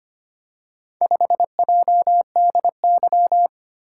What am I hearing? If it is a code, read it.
Morse code, heard as a keyed tone, about 25 wpm, 700 Hz. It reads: 5JDY
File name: practice / 25wpm / Four Letter-Number Combinations - 25wpm